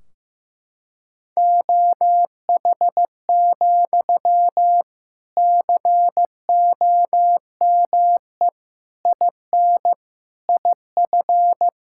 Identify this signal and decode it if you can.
Morse code, heard as a keyed tone, about 16 wpm, 700 Hz. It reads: OH, COME IN IF
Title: Sentences from Top 1600 Words 15wpm